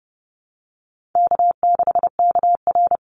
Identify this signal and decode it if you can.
Morse code, heard as a keyed tone, about 30 wpm, 700 Hz. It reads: K6KR
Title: Easy Call Signs - 30wpm